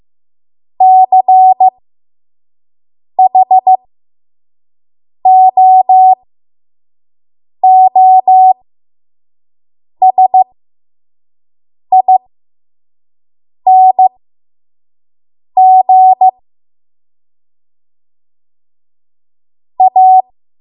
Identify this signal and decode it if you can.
Morse code, heard as a keyed tone, about 15 wpm, spaced as for 5 wpm, 750 Hz. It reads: CHOOSING A
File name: Morse / 05WPM